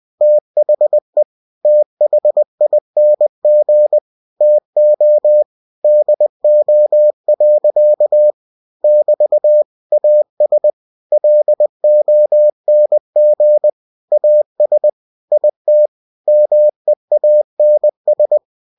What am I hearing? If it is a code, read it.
Morse code, heard as a keyed tone, about 20 wpm, 600 Hz. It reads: THE THING TO DO. <BT> AS LONG AS IT MEANS
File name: cw-books / winnie-the-pooh / winnie-20wpm-page-0030